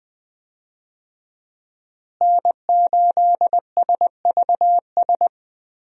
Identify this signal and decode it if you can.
Morse code, heard as a keyed tone, about 20 wpm, 700 Hz. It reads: N8SVS
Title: USA Call Signs - 5 Characters - Repeated 2x - 20wpm